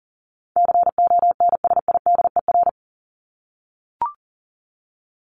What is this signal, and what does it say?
Morse code, heard as a keyed tone, about 40 wpm, 700 Hz. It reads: CONSIDER
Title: ED Suffix - Top 100 Words with Base Words 40wpm